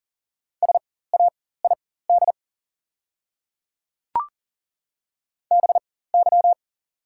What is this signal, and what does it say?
Morse code, heard as a keyed tone, about 38 wpm, spaced as for 18 wpm, 700 Hz. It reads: SAID BY
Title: 4x Character Spacing - Top 100 Words - 40wpm